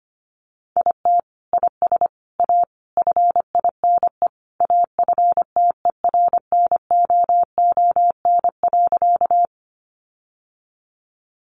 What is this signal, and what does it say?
Morse code, heard as a keyed tone, about 25 wpm, 700 Hz. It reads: IT IS A FINE AFTERNOON.
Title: Sentences from Top 1500 Words 25wpm